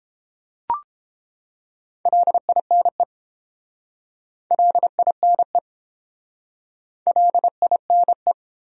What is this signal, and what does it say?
Morse code, heard as a keyed tone, about 29 wpm, 700 Hz. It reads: LINE LINE LINE